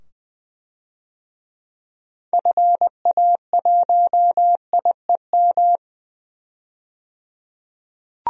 Morse code, heard as a keyed tone, about 20 wpm, 700 Hz. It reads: FA1IEM
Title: DXCC-100 Call Signs - 6 Characters - 20wpm